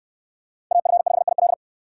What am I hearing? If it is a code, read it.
Morse code, heard as a keyed tone, about 68 wpm, 700 Hz. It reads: AF5IL